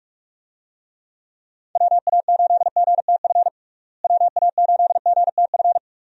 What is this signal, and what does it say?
Morse code, heard as a keyed tone, about 47 wpm, 700 Hz. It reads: WA8GTF WA8GTF
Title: USA Call Signs - 6 Characters - Repeated 3x - 45wpm